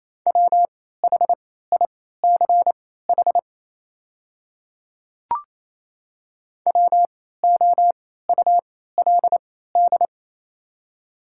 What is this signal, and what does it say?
Morse code, heard as a keyed tone, about 28 wpm, spaced as for 16 wpm, 700 Hz. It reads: WHICH WOULD
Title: 3x Character Spacing - Top 100 Words - 28wpm